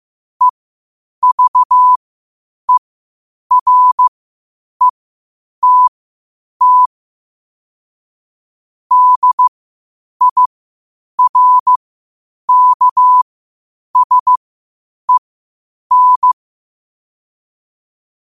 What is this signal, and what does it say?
Morse code, heard as a keyed tone, about 15 wpm, spaced as for 8 wpm, 1000 Hz. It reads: EVERETT DIRKSEN
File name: qotdfiles / 8wpm